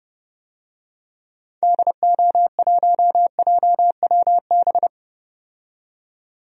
Morse code, heard as a keyed tone, about 30 wpm, 700 Hz. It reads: DO1JWB
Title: DXCC-100 Call Signs - 6 Characters - 30wpm